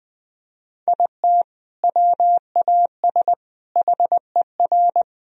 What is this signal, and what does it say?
Morse code, heard as a keyed tone, about 20 wpm, 700 Hz. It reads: IT WAS HER
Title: Sentences from Top 1000 Words - Review Part 4 - 20wpm